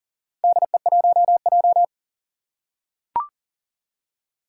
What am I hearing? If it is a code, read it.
Morse code, heard as a keyed tone, about 40 wpm, 700 Hz. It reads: DE1J
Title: Four Challenging Letter-Number Combinations 40wpm